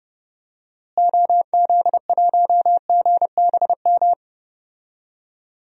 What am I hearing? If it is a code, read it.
Morse code, heard as a keyed tone, about 30 wpm, 700 Hz. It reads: OZ1GBM
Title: DXCC-100 Call Signs - 6 Characters - Repeated 3x - 30wpm